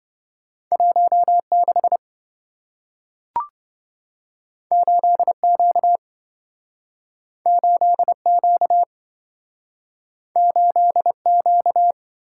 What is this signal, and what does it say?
Morse code, heard as a keyed tone, about 27 wpm, 700 Hz. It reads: 16 8Q 8Q 8Q